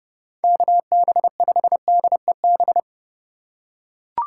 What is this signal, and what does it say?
Morse code, heard as a keyed tone, about 28 wpm, 700 Hz. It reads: KB5DEB